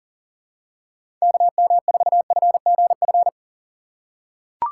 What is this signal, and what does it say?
Morse code, heard as a keyed tone, about 40 wpm, 700 Hz. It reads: KM4FGF
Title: USA Call Signs - 6 Characters - 4x Character Spacing - 40wpm